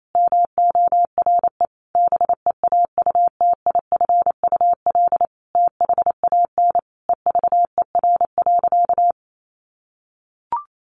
Morse code, heard as a keyed tone, about 28 wpm, 700 Hz. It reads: MORE BEAUTIFUL THAN EVER.
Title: Sentences from Top 1200 Words 28wpm